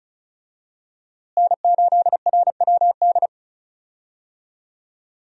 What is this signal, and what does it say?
Morse code, heard as a keyed tone, about 34 wpm, 700 Hz. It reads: N8RWD
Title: USA Call Signs - 5 Characters - Repeated 2x - 35wpm